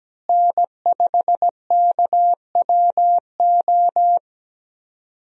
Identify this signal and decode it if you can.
Morse code, heard as a keyed tone, about 17 wpm, 700 Hz. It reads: N5KWO
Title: USA Call Signs - 5 Characters - Repeated 3x - 17wpm